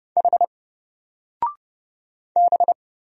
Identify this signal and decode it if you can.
Morse code, heard as a keyed tone, about 30 wpm, 700 Hz. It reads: H B
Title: Single Character (Letter:Number:Punctuation:Prosign) 30wpm